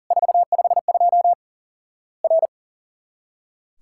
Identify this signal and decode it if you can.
Morse code, heard as a keyed tone, about 40 wpm, 700 Hz. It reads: 452 R
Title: Worked All Europe DX Contest - QTC Exchange - 40wpm